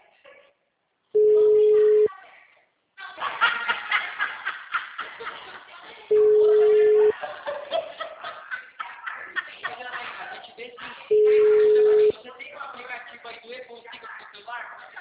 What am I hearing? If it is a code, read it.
Ringback tone (Europe, 425 Hz), with speech.